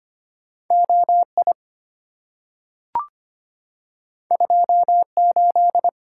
Morse code, heard as a keyed tone, about 25 wpm, 700 Hz. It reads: OI 28
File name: Two Letter-Number Combinations - 25wpm